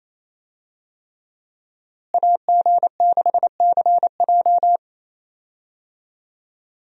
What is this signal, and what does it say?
Morse code, heard as a keyed tone, about 28 wpm, 700 Hz. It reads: AG6CJ
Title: USA Call Signs - 5 Characters - Repeated 3x - 28wpm